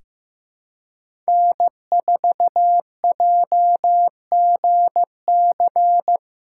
Morse code, heard as a keyed tone, about 15 wpm, 700 Hz. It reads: N4JGC
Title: USA Callsign DE Callsign 15wpm